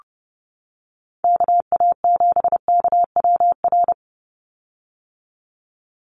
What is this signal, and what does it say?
Morse code, heard as a keyed tone, about 30 wpm, 700 Hz. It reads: KA7KWR
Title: USA Call Signs 30wpm